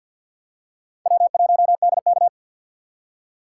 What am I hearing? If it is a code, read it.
Morse code, heard as a keyed tone, about 53 wpm, 700 Hz. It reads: W1DK
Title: USA Call Signs - 3x Character Spacing - 50wpm